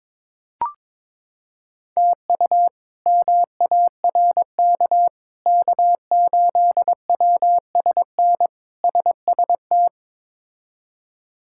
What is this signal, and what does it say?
Morse code, heard as a keyed tone, about 22 wpm, 700 Hz. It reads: TU MARK K8WSN SST
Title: K1USN SST QSO Practice - 22wpm